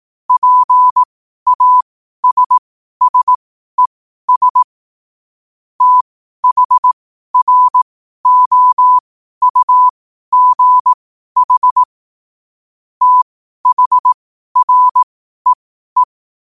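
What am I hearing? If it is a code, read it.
Morse code, heard as a keyed tone, about 18 wpm, spaced as for 12 wpm, 1000 Hz. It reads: PASSES THROUGH THREE
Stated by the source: Morse code quote of the day at 12 wpm.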